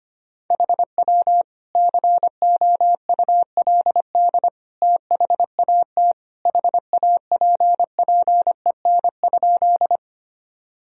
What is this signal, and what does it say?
Morse code, heard as a keyed tone, about 25 wpm, 700 Hz. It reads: HW COULD THAT HAPPEN?
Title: Sentences from Top 400 Words and Abbreviations 25wpm